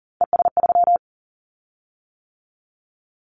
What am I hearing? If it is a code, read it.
Morse code, heard as a keyed tone, about 40 wpm, 700 Hz. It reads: ES3
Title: Callsign Prefixes for Top 100 DXCC - 40wpm